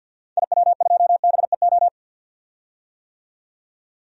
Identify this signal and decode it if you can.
Morse code, heard as a keyed tone, about 50 wpm, 700 Hz. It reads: IW2BEO
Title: DXCC-100 Call Signs - 6 Characters - 50wpm